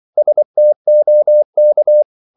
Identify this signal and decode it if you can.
Morse code, heard as a keyed tone, about 24 wpm, 600 Hz. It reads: STOK
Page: Korte danske ord 24wpm